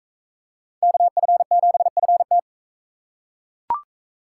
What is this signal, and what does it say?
Morse code, heard as a keyed tone, about 42 wpm, 700 Hz. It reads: KF7FT